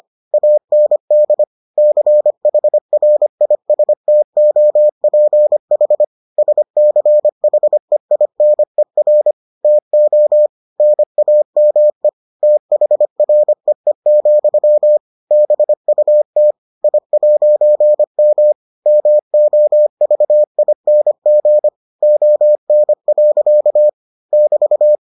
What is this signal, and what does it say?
Morse code, heard as a keyed tone, about 25 wpm, 600 Hz. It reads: AND CHRISTOPH SCHEINER TO NAME THREE, BUT I'M MOVING ON. <BT>